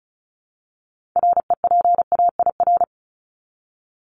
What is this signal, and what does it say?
Morse code, heard as a keyed tone, about 34 wpm, 700 Hz. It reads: REPAIR
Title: Top 1501-1750 Words 35wpm